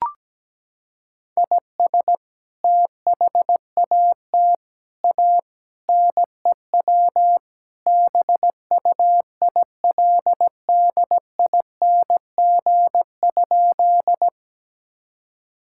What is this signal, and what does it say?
Morse code, heard as a keyed tone, about 17 wpm, 700 Hz. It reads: IS THAT A NEW BUILDING?
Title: Sentences from Top 1000 Words - Review Part 5 - 17wpm